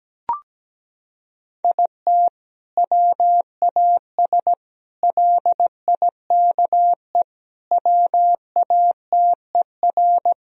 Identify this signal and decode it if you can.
Morse code, heard as a keyed tone, about 17 wpm, 700 Hz. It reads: IT WAS LIKE WATER
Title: Sentences from Top 1000 Words - Review Part 4 - 17wpm